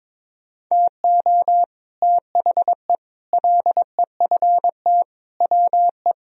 Morse code, heard as a keyed tone, about 22 wpm, 700 Hz. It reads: TO THE LEFT WE
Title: Sentences from Top 1000 Words - Review Part 1 - 22wpm